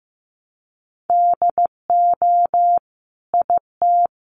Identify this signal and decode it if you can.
Morse code, heard as a keyed tone, about 14 wpm, 700 Hz. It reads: DO IT
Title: Sentences from Top 1300 Words 15wpm